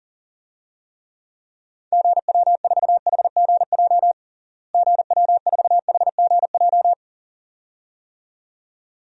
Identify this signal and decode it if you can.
Morse code, heard as a keyed tone, about 40 wpm, 700 Hz. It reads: GW4HGJ GW4HGJ
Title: DXCC-100 Call Signs - 6 Characters - Repeated 2x - 40wpm